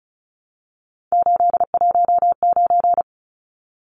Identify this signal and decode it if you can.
Morse code, heard as a keyed tone, about 34 wpm, 700 Hz. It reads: 819
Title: Numbers - 3-Digit 35wpm